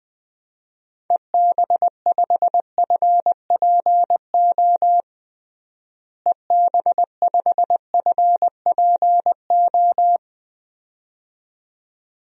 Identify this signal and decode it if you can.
Morse code, heard as a keyed tone, about 20 wpm, 700 Hz. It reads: EB5FPO EB5FPO
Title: DXCC-100 Call Signs - 6 Characters - Repeated 2x - 20wpm